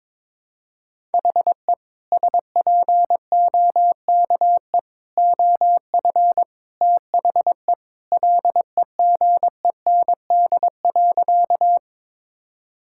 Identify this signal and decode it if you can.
Morse code, heard as a keyed tone, about 22 wpm, 700 Hz. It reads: HE SPOKE OF THE LEGEND.
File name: practice / 22wpm / Sentences with 2-Word Phrases - 22wpm